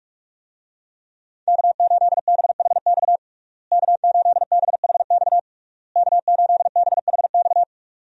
Morse code, heard as a keyed tone, about 46 wpm, 700 Hz. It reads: K8BHX K8BHX K8BHX
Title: USA Call Signs - 5 Characters - Repeated 3x - 45wpm